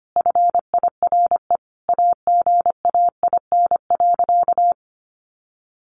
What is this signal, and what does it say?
Morse code, heard as a keyed tone, about 25 wpm, 700 Hz. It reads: FIRE AGAIN.
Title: Sentences from Top 1400 Words 25wpm